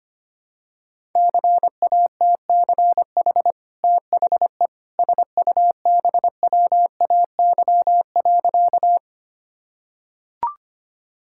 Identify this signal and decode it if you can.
Morse code, heard as a keyed tone, about 25 wpm, 700 Hz. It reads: CATCH THE SUBWAY.
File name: Easy Sentences with Hints - 25wpm